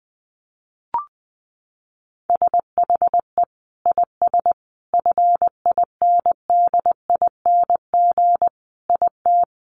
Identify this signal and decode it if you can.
Morse code, heard as a keyed tone, about 20 wpm, 700 Hz. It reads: SHE IS FINDING IT
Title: Sentences from Top 1100 Words 20wpm